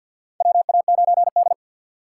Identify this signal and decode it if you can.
Morse code, heard as a keyed tone, about 48 wpm, 700 Hz. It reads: WA9D
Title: USA Call Signs - 4 Characters - 2x Character Spacing - 50wpm